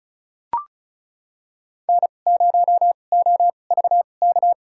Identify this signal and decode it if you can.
Morse code, heard as a keyed tone, about 34 wpm, spaced as for 25 wpm, 700 Hz. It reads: N0OVK
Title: USA Call Signs - 2x Character Spacing - 35wpm